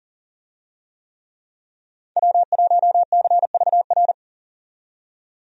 Morse code, heard as a keyed tone, about 40 wpm, 700 Hz. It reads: W1CVR
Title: USA Call Signs - 5 Characters - 40wpm